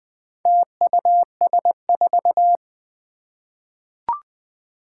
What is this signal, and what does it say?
Morse code, heard as a keyed tone, about 20 wpm, 700 Hz. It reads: TUS4
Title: Four Challenging Letter-Number Combinations 20wpm